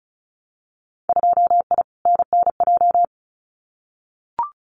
Morse code, heard as a keyed tone, about 34 wpm, 700 Hz. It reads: 2I NNJ
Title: Winter Field Day Exchanges Encore 35wpm